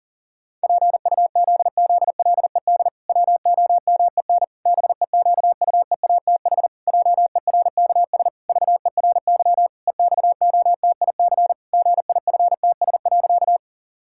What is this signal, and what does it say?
Morse code, heard as a keyed tone, about 40 wpm, 700 Hz. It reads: PUZZLED WOMEN BEQUEATH JERKS VERY EXOTIC GIFTS.
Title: Pangram Challenge v2 - 40wpm